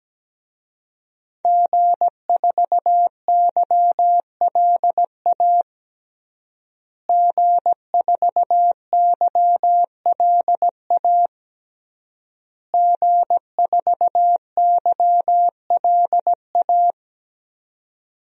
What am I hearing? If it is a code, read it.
Morse code, heard as a keyed tone, about 17 wpm, 700 Hz. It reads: G4YLA G4YLA G4YLA
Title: DXCC-100 Call Signs - 5 Characters - Repeated 3x - 17wpm